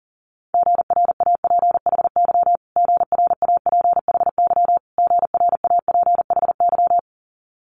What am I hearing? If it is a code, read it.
Morse code, heard as a keyed tone, about 41 wpm, 690 Hz. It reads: GRAPHY GRAPHY GRAPHY
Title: Multi-letter Phonemes Repeated 3x 40wpm